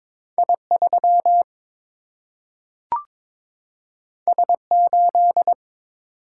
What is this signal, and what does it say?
Morse code, heard as a keyed tone, about 22 wpm, 700 Hz. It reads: I 3 S 8
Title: Single-Letter Single-Number 22wpm